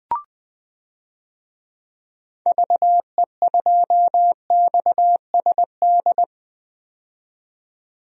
Morse code, heard as a keyed tone, about 20 wpm, 700 Hz. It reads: VE2XSD
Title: DXCC-100 Call Signs - 6 Characters - 20wpm